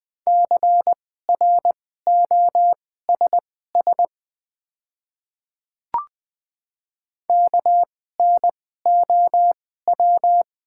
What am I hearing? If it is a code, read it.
Morse code, heard as a keyed tone, about 20 wpm, spaced as for 14 wpm, 700 Hz. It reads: CROSS KNOW
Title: 2x Character Spacing - Top 300 Words - 20wpm